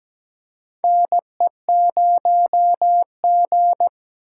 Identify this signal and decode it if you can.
Morse code, heard as a keyed tone, about 17 wpm, 700 Hz. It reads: NE0G
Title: USA Call Signs - 4 Characters - 3x Character Spacing - 17wpm